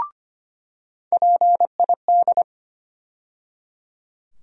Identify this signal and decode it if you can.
Morse code, heard as a keyed tone, about 25 wpm, 700 Hz. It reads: PID
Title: USA Call Sign Postfixes - 3 Letters 25wpm